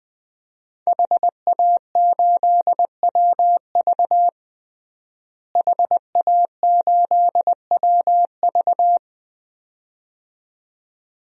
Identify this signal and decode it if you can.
Morse code, heard as a keyed tone, about 20 wpm, 700 Hz. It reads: HA8WV HA8WV
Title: DXCC-100 Call Signs - 5 Characters - Repeated 2x - 20wpm